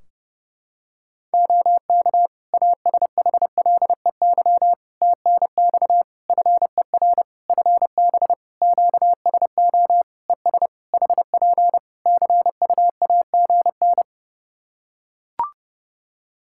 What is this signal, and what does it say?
Morse code, heard as a keyed tone, about 30 wpm, 700 Hz. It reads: OK ASHLEY TNX FER FB QSO ES HP CUAGN
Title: Practice QSO - Exchange 4 - Ending - 30wpm